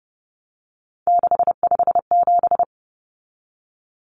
Morse code, heard as a keyed tone, about 30 wpm, 700 Hz. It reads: 657
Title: Common Cut Numbers - 3-Digit 30wpm